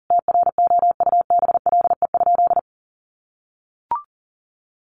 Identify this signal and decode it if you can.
Morse code, heard as a keyed tone, about 40 wpm, 700 Hz. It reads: TROUBLE?
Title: Sentences from Top 1200 Words 40wpm